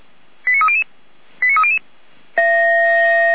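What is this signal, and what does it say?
Five-tone selective call (ZVEI), heard as quick runs of five tones, tones of about 70 ms. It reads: 89300 89300
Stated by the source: ZVEI five-tone selective call received by SDR.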